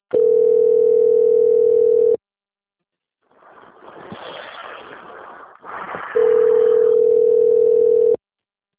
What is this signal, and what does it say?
Ringback tone (North America), 440 + 480 Hz.